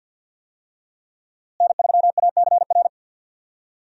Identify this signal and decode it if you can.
Morse code, heard as a keyed tone, about 52 wpm, 700 Hz. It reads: N3ACR
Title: USA Call Signs - 5 Characters - Repeated 3x - 50wpm